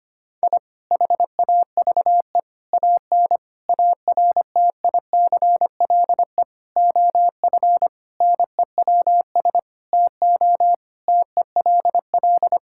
Morse code, heard as a keyed tone, about 25 wpm, 700 Hz. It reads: I HAVE AN ARTICLE OF NEWS TO TELL
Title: Sentences from Top 1000 Words - Review Part 5 - 25wpm